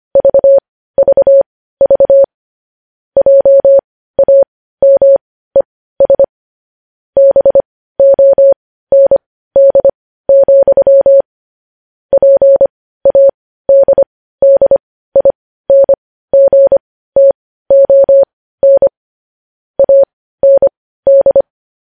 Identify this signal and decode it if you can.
Morse code, heard as a keyed tone, about 25 wpm, spaced as for 15 wpm, 550 Hz. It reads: VVV JAMES BOND, PADDINGTON AND
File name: podcast / News headlines-1x-25x15